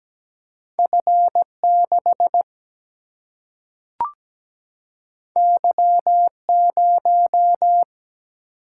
Morse code, heard as a keyed tone, about 17 wpm, 700 Hz. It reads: F 6 Y 0
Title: Single-Letter Single-Number 17wpm